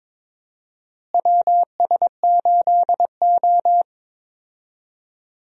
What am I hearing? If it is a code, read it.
Morse code, heard as a keyed tone, about 21 wpm, 700 Hz. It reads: WS8O
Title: USA Call Signs - 4 Characters - Repeated 2x - 22wpm